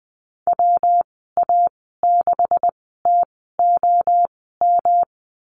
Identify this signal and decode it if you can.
Morse code, heard as a keyed tone, about 20 wpm, spaced as for 14 wpm, 700 Hz. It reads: WA6TOM
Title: 128 Lesson 21g - Number 6 - Call Signs 20wpm